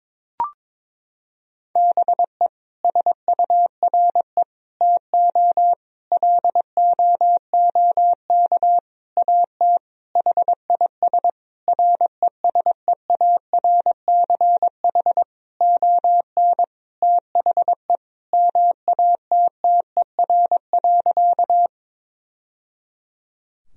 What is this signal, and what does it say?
Morse code, heard as a keyed tone, about 21 wpm, 700 Hz. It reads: BE SURE TO LOOK AT HIS RESEARCH ON THE MATTER.
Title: Sentences from Top 1000 Words - Review Part 4 - 22wpm